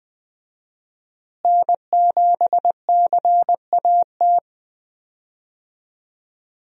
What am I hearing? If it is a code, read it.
Morse code, heard as a keyed tone, about 20 wpm, 700 Hz. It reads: N7CAT
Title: USA Call Signs - 5 Characters - Repeated 3x - 20wpm